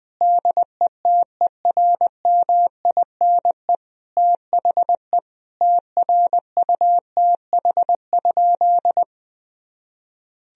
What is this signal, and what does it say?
Morse code, heard as a keyed tone, about 20 wpm, 700 Hz. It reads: DETERMINE THE TRUTH?
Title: Sentences from Top 1000 Words - Review Part 4 - 20wpm